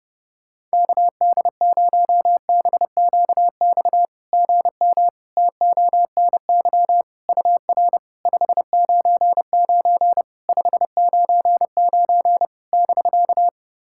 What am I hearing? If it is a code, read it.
Morse code, heard as a keyed tone, about 30 wpm, 690 Hz. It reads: KD0BQX GM TONY UR 599 599 <BK>
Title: SOTA QSO Practice 30wpm